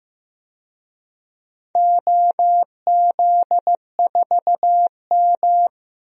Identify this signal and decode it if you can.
Morse code, heard as a keyed tone, about 15 wpm, 700 Hz. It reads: OZ4M
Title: DXCC-100 Call Signs - 4 Characters - 15wpm